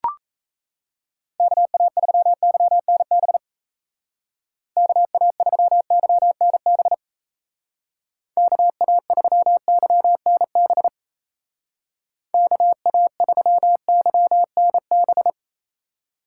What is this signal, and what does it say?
Morse code, heard as a keyed tone, about 37 wpm, 700 Hz. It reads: KA3YNB KA3YNB KA3YNB KA3YNB